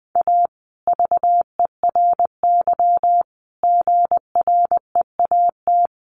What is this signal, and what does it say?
Morse code, heard as a keyed tone, about 20 wpm, 700 Hz. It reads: A VERY GREAT
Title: Sentences from Top 1400 Words 20wpm